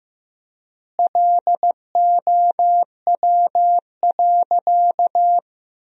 Morse code, heard as a keyed tone, about 15 wpm, 700 Hz. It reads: LOW.
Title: 3x Word-Spacing - Sentences from Top 500 Words - 15wpm